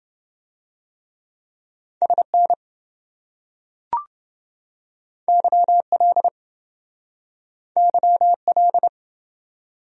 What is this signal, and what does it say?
Morse code, heard as a keyed tone, about 28 wpm, 700 Hz. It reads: SN YL YL